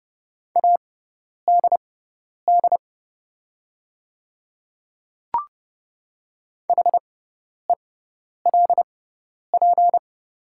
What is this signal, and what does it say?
Morse code, heard as a keyed tone, about 30 wpm, spaced as for 10 wpm, 690 Hz. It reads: ADD HELP
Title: 6x Character Spacing - Top 200 Words - 30wpm